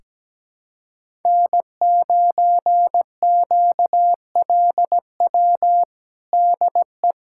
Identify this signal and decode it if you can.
Morse code, heard as a keyed tone, about 17 wpm, 700 Hz. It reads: N9QLW DE
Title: USA Callsign DE Callsign - 5 Characters - 17wpm